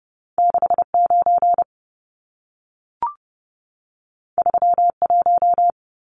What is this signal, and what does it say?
Morse code, heard as a keyed tone, about 30 wpm, 700 Hz. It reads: 69 31
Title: Numbers - 2-Digit 30wpm